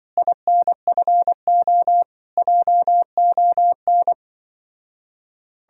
Morse code, heard as a keyed tone, about 24 wpm, 700 Hz. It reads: INFO JON
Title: Speed-Racing Common QSO Phrases - 20wpm